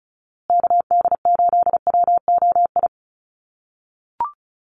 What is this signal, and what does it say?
Morse code, heard as a keyed tone, about 34 wpm, 700 Hz. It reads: KD8WOI
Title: USA Call Signs Repeated 2x 35wpm